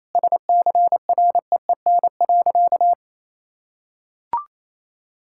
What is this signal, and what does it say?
Morse code, heard as a keyed tone, about 28 wpm, 700 Hz. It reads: SCREEN.
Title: Sentences from Top 1000 Words - Review Part 6 - 28wpm